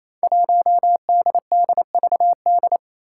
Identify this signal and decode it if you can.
Morse code, heard as a keyed tone, about 27 wpm, 700 Hz. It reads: 1DDVD
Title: Five Challenging Letter-Number Combinations 28wpm